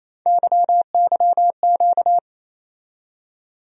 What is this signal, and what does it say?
Morse code, heard as a keyed tone, about 27 wpm, 700 Hz. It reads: YYQ
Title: USA Call Sign Postfixes - 3 Letters 28wpm